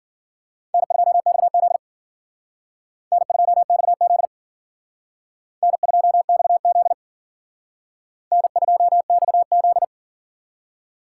Morse code, heard as a keyed tone, about 51 wpm, 700 Hz. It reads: N2XZ N2XZ N2XZ N2XZ